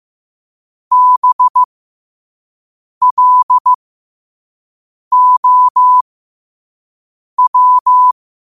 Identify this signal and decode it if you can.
Morse code, heard as a keyed tone, about 15 wpm, spaced as for 5 wpm, 1000 Hz. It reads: BLOW
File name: qotdfiles / 5wpm